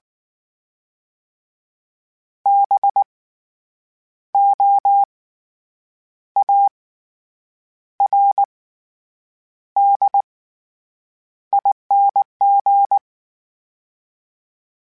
Morse code, heard as a keyed tone, about 19 wpm, spaced as for 7 wpm, 800 Hz. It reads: BOARDING
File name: Special INGs / Special INGs-mp3s / SlowToFaster ing @18